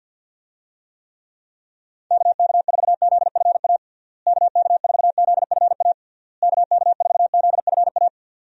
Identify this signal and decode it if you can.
Morse code, heard as a keyed tone, about 51 wpm, 700 Hz. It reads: KK4ZFA KK4ZFA KK4ZFA
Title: USA Call Signs - 6 Characters - Repeated 3x - 50wpm